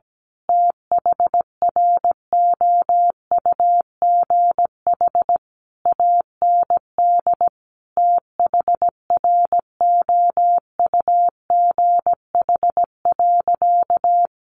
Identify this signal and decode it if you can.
Morse code, heard as a keyed tone, about 17 wpm, 700 Hz. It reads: THROUGH AND THROUGH.
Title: Sentences from Top 1400 Words 17wpm